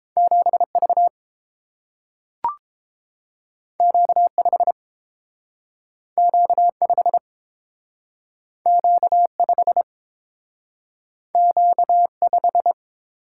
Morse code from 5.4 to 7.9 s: Q 5

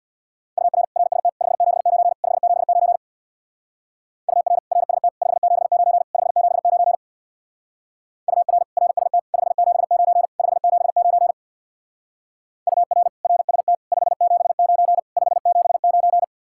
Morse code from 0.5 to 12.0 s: UR RST 579 579 UR RST 579 579 UR RST 579 579

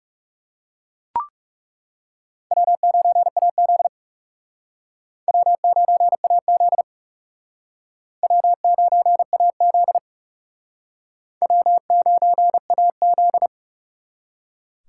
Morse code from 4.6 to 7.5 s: W9AZ

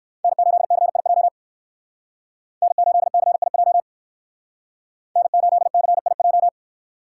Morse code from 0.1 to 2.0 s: N8CIJ

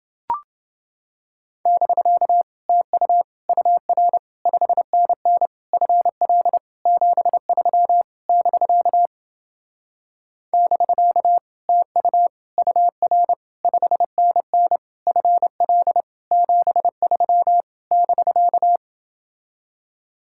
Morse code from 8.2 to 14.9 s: <BK> <BK> TU UR 5NN

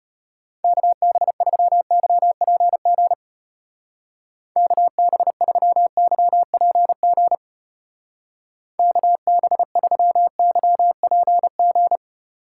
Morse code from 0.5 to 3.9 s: KB3YPG

